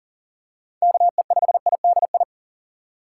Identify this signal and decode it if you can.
Morse code, heard as a keyed tone, about 40 wpm, 700 Hz. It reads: KE5IDI
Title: USA Call Signs - 6 Characters - 2x Character Spacing - 40wpm